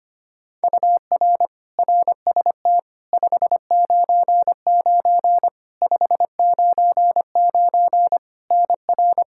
Morse code, heard as a keyed tone, about 25 wpm, 700 Hz. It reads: UR RST 599 599 NR
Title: Practice QSO - Exchange 1 - Beginning - 25wpm